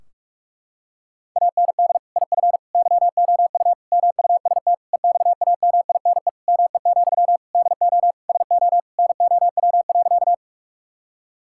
Morse code, heard as a keyed tone, about 46 wpm, 700 Hz. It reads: AND IF YOU MUST EXAMINE ME, DO SO NOW.
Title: Sentences from Top 1600 Words 45wpm